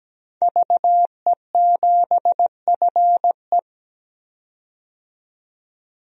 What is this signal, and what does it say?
Morse code, heard as a keyed tone, about 17 wpm, 700 Hz. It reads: VE7FE
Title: DXCC-100 Call Signs - 5 Characters - 17wpm